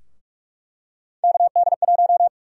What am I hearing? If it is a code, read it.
Morse code, heard as a keyed tone, about 44 wpm, 700 Hz. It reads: KD1
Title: USA Call Sign Prefixes - Repeated 2x - 45wpm